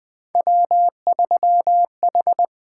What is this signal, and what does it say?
Morse code, heard as a keyed tone, about 20 wpm, 700 Hz. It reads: W3H
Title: USA Call Signs - 5 Characters - 8x Character Spacing - 20wpm